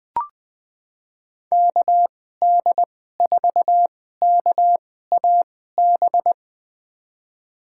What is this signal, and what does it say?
Morse code, heard as a keyed tone, about 20 wpm, spaced as for 14 wpm, 700 Hz. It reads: KD4KAB
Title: USA Call Signs - 6 Characters - 2x Character Spacing - 20wpm